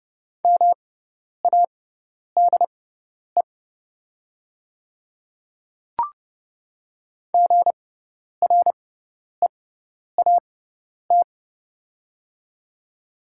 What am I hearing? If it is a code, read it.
Morse code, heard as a keyed tone, about 31 wpm, spaced as for 10 wpm, 700 Hz. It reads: MADE GREAT
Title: 6x Character Spacing - Top 200 Words - 30wpm